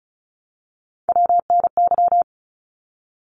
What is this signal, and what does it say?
Morse code, heard as a keyed tone, about 34 wpm, 700 Hz. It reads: WNY
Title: ARRL Sections 35wpm